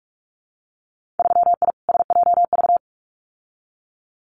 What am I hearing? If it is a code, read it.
Morse code, heard as a keyed tone, about 46 wpm, 700 Hz. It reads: 3I SJV
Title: Winter Field Day Exchanges Encore 45wpm